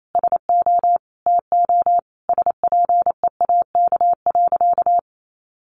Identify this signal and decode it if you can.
Morse code, heard as a keyed tone, about 28 wpm, 700 Hz. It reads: SO TO SPEAK.
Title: Sentences from Top 1400 Words 28wpm